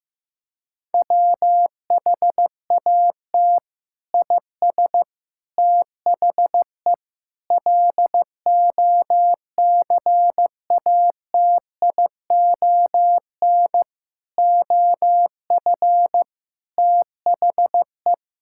Morse code, heard as a keyed tone, about 15 wpm, 700 Hz. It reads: WHAT IS THE LOCATION OF THE
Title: Sentences from Top 1000 Words - Review Part 6 - 15wpm